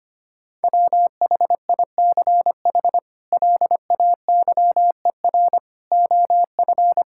Morse code, heard as a keyed tone, about 25 wpm, 700 Hz. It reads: WHICH LAYER OF
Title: Sentences from Top 1600 Words 25wpm